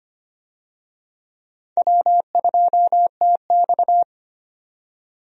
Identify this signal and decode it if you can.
Morse code, heard as a keyed tone, about 25 wpm, 700 Hz. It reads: W2TX
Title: USA Call Signs - 4 Characters - 25wpm